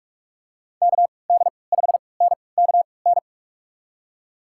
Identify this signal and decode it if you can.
Morse code, heard as a keyed tone, about 47 wpm, spaced as for 26 wpm, 700 Hz. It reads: KD5NKN
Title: USA Call Signs - 6 Characters - 3x Character Spacing - 45wpm